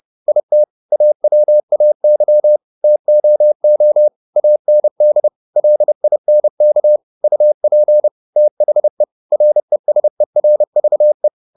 Morse code, heard as a keyed tone, about 30 wpm, 600 Hz. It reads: IT AWAY TOO AND LINK UP THE RESERVE
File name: the old man and the sea / the old man and the sea 1 30wpm